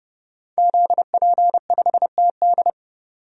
Morse code, heard as a keyed tone, about 32 wpm, 700 Hz. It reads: ZP5TD
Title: DXCC-100 Call Signs - 5 Characters - 30wpm